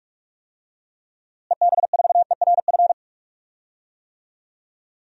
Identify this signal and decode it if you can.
Morse code, heard as a keyed tone, about 46 wpm, 700 Hz. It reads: EB4ERF
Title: DXCC-100 Call Signs - 6 Characters - Repeated 2x - 45wpm